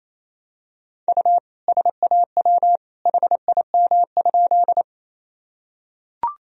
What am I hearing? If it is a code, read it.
Morse code, heard as a keyed tone, about 28 wpm, 700 Hz. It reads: U SAW HIM?
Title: Sentences from Top 300 Words and Abbreviations 28wpm